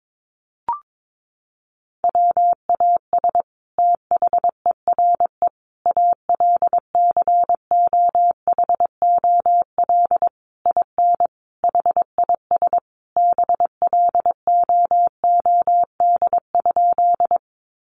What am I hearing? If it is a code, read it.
Morse code, heard as a keyed tone, about 22 wpm, 700 Hz. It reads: WAS THERE ALCOHOL IN HIS BLOOD?
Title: Sentences from Top 1400 Words 22wpm